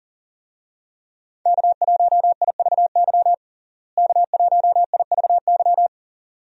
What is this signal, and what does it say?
Morse code, heard as a keyed tone, about 37 wpm, 700 Hz. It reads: K1IVY K1IVY
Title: USA Call Signs - 5 Characters - Repeated 2x - 40wpm